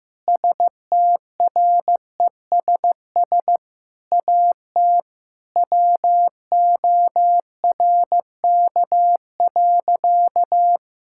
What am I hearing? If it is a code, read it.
Morse code, heard as a keyed tone, about 15 wpm, 700 Hz. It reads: STRESS AT WORK.
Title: Sentences from Top 1000 Words - Review Part 6 - 15wpm